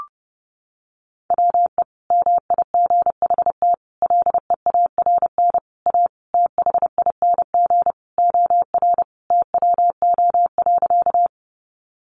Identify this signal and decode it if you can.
Morse code, heard as a keyed tone, about 30 wpm, 700 Hz. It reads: WE MIGHT LEARN A THING OR TWO.
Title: Sentences from Top 300 Words 30wpm